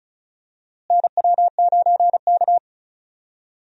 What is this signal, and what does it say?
Morse code, heard as a keyed tone, about 36 wpm, 700 Hz. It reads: NW9K
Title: USA Call Signs - 4 Characters - Repeated 2x - 35wpm